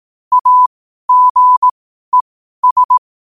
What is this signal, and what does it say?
Morse code, heard as a keyed tone, about 18 wpm, spaced as for 12 wpm, 1000 Hz. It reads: AGES
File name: qotdfiles / 12wpm